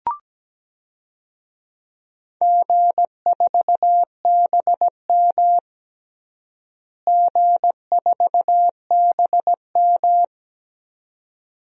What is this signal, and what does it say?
Morse code, heard as a keyed tone, about 17 wpm, 700 Hz. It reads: G4BM G4BM
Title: DXCC-100 Call Signs - 4 Characters - Repeated 2x - 17wpm